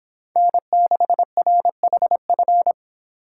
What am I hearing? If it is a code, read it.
Morse code, heard as a keyed tone, about 25 wpm, 700 Hz. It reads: N6RHF